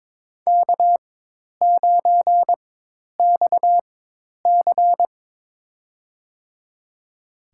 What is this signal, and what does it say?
Morse code, heard as a keyed tone, about 21 wpm, spaced as for 10 wpm, 700 Hz. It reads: K9XC
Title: USA Call Signs - 4x Character Spacing - 22wpm